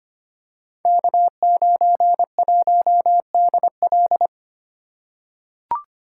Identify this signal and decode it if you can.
Morse code, heard as a keyed tone, about 25 wpm, 700 Hz. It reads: K91DL
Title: Code Groups - 5 Letter:Numbers - 25wpm